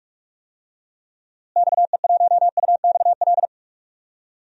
Morse code, heard as a keyed tone, about 43 wpm, 700 Hz. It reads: XE1UXL
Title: DXCC-100 Call Signs - 6 Characters - Repeated 2x - 45wpm